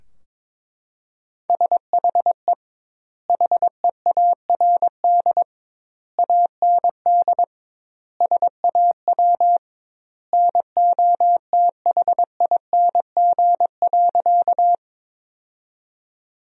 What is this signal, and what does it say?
Morse code, heard as a keyed tone, about 21 wpm, 700 Hz. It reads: SHE HEARD AND SAW NOTHING.
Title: 2x Word-Spacing - Sentences from Top 500 Words - 22wpm